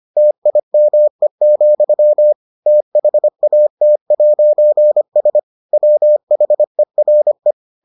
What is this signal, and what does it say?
Morse code, heard as a keyed tone, about 25 wpm, 600 Hz. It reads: TIME, THAT'S WHERE